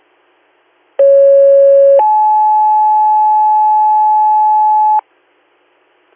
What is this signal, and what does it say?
Two-tone page, heard as a one-second tone, then a three-second tone, about 550 then 860 Hz. It is heard from 1.0 to 5.0 s.